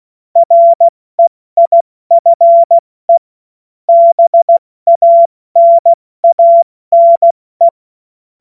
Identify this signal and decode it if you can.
Morse code, heard as a keyed tone, about 16 wpm, spaced as for 14 wpm, 680 Hz. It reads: REIFE BANANE
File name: Morsekurs / Witze / Witz 05 Dezember 2023 70 F80